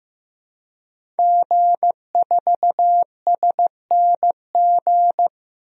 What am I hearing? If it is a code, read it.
Morse code, heard as a keyed tone, about 15 wpm, 700 Hz. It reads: G4SNG
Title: DXCC-100 Call Signs - 5 Characters - 15wpm